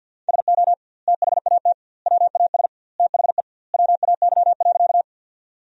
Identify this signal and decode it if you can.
Morse code, heard as a keyed tone, about 50 wpm, 700 Hz. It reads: SO THAT WAS THE WAY.
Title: 2x Word-Spacing - Sentences from Top 100 Words - 50wpm